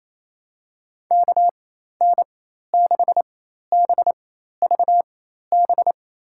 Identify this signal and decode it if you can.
Morse code, heard as a keyed tone, about 27 wpm, spaced as for 13 wpm, 700 Hz. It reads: KN6BVB
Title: USA Call Signs - 4x Character Spacing - 28wpm